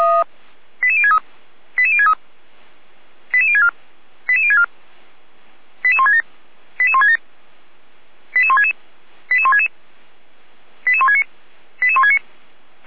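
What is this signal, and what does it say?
Five-tone selective call (ZVEI), heard as quick runs of five tones, tones of about 70 ms. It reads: 88063 88063 88064 88064 88167 88167 88166 88166 88169 88169
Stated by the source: ZVEI five-tone call received on an SDR.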